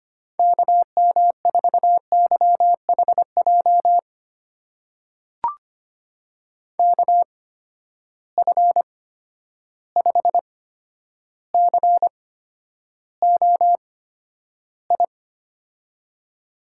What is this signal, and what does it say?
Morse code, heard as a keyed tone, about 25 wpm, spaced as for 11 wpm, 700 Hz. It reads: KM4YHJ KF5COI
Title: USA Call Signs - 6 Characters - 8x Character Spacing - 25wpm